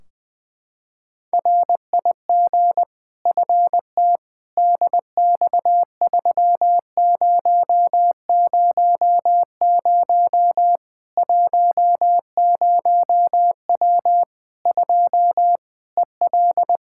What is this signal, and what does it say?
Morse code, heard as a keyed tone, about 20 wpm, 700 Hz. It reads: RIG FT DX3000 10W 2 EL
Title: Practice QSO - Exchange 2 - Info - 20wpm